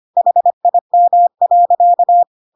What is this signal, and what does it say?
Morse code, heard as a keyed tone, about 25 wpm, 700 Hz. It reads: HIM.
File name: Pinocchio / Pinocchio ch3 25 25 ik1hge